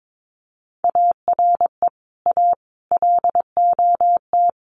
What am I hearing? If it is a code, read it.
Morse code, heard as a keyed tone, about 22 wpm, 700 Hz. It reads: ARE A LOT
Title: Sets of 5 Words Encore2 22wpm